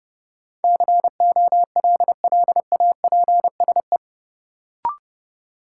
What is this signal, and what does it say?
Morse code, heard as a keyed tone, about 31 wpm, 700 Hz. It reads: COLLAPSE
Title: Sets of 5 Words Encore - New Word Drill - 30wpm